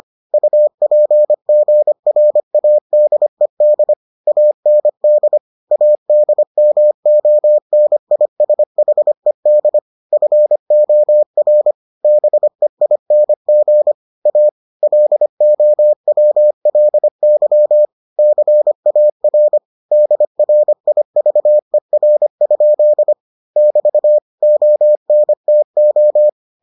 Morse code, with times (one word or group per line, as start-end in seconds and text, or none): 0.3-3.9 UPGRADED
4.3-5.4 AND
5.7-9.8 ADMONISHED
10.1-11.7 FOR
12.0-13.9 BEING
14.3-14.5 A
14.8-17.9 LOWLY
18.2-19.6 CAR
19.9-23.1 DRIVER?
23.6-24.2 <BT>
24.4-26.3 ONTO